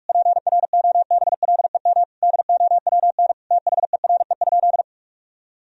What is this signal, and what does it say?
Morse code, heard as a keyed tone, about 46 wpm, 700 Hz. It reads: PROBLEM DOWN THERE?
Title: Sentences from Top 1000 Words - Review Part 4 - 45wpm